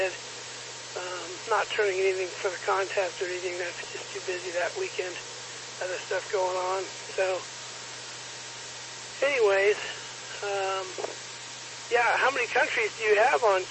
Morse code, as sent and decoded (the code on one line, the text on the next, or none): none